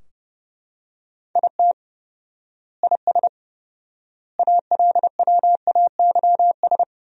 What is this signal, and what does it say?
Morse code, heard as a keyed tone, about 30 wpm, 700 Hz. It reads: IT IS ALWAYS
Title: 4x Word-Spacing - Sentences from Top 300 Words - 30wpm